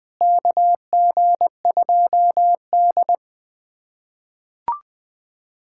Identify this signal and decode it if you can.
Morse code, heard as a keyed tone, about 20 wpm, 700 Hz. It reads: KG2D
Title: USA Callsign DE Callsign - 4 Characters - 20wpm